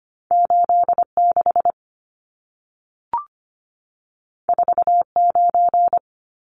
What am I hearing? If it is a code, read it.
Morse code, heard as a keyed tone, about 25 wpm, 700 Hz. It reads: 86 49
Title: Numbers - 2-Digit 25wpm